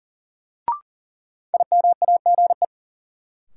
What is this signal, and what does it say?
Morse code, heard as a keyed tone, about 38 wpm, 700 Hz. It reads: IMAGE
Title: Sentences from Top 700 Words - New Word Drill - 40wpm